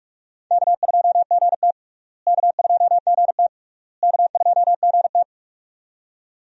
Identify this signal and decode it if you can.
Morse code, heard as a keyed tone, about 44 wpm, 700 Hz. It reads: K2GT K2GT K2GT
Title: USA Call Signs - 4 Characters - Repeated 3x - 45wpm